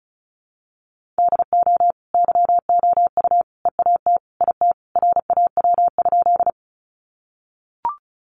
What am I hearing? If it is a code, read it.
Morse code, heard as a keyed tone, about 34 wpm, 700 Hz. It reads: DO YOU EAT IT RAW?
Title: Sentences from Top 1500 Words 35wpm